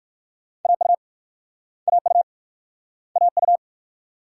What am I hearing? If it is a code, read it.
Morse code, heard as a keyed tone, about 56 wpm, 690 Hz. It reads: AU AU AU
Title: Speed-Racing Multi-letter Phonemes - 45wpm